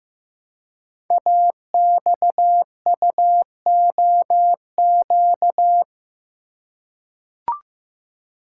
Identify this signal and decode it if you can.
Morse code, heard as a keyed tone, about 15 wpm, 700 Hz. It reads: AXUOQ
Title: Code Groups - 5 Letters - 15wpm